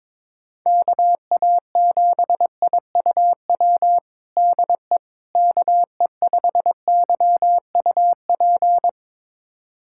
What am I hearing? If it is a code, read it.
Morse code, heard as a keyed tone, about 22 wpm, 700 Hz. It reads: KA7IUW DE KE5YUP
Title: USA Callsign DE Callsign - 6 Characters - 22wpm